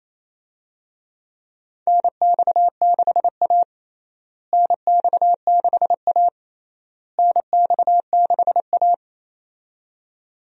Morse code, heard as a keyed tone, about 28 wpm, 700 Hz. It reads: NX6A NX6A NX6A
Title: USA Call Signs - 4 Characters - Repeated 3x - 28wpm